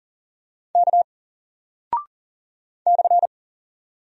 Morse code, heard as a keyed tone, about 39 wpm, 700 Hz. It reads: K /
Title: Single Character (Letter:Number:Punctuation:Prosign) 40wpm